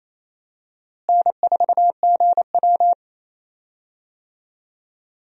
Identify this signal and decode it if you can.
Morse code, heard as a keyed tone, about 28 wpm, 700 Hz. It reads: N4GW
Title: USA Call Signs - 4 Characters - Repeated 2x - 28wpm